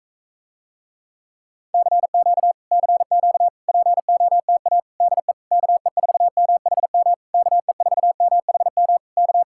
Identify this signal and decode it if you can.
Morse code, heard as a keyed tone, about 46 wpm, 700 Hz. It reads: CQ CQ POTA DE KE4MHM KE4MHM K